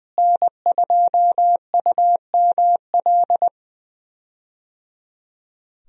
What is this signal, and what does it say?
Morse code, heard as a keyed tone, about 20 wpm, 700 Hz. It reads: N2UML
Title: USA Call Signs - 5 Characters - Repeated 3x - 20wpm